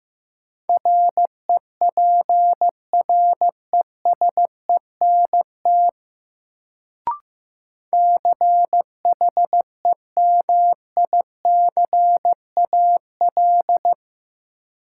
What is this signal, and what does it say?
Morse code, heard as a keyed tone, about 15 wpm, 700 Hz. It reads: REPRESENT CHEMICAL
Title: Sentences from Top 1200 Words - New Word Drill - 15wpm